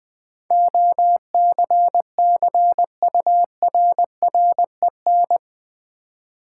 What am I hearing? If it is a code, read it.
Morse code, heard as a keyed tone, about 20 wpm, 700 Hz. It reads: OCCURREN
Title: Build A Word - 4 Letters At A Time - 10-Letter Words - 20wpm